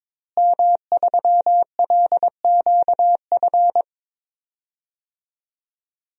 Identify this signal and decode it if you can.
Morse code, heard as a keyed tone, about 22 wpm, 700 Hz. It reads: M3LQF
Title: DXCC-100 Call Signs - 5 Characters - Repeated 2x - 22wpm